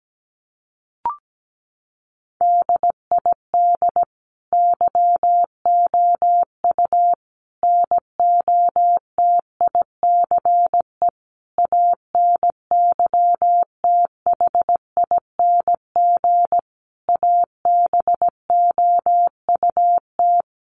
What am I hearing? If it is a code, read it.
Morse code, heard as a keyed tone, about 17 wpm, 700 Hz. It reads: DID YOU NOTICE ANYTHING ABOUT
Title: Sentences from Top 1000 Words 17wpm